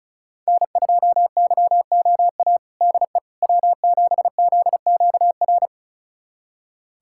Morse code, heard as a keyed tone, about 36 wpm, 700 Hz. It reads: N2YOA DE W7ZQR
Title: USA Callsign DE Callsign - 5 Characters - 35wpm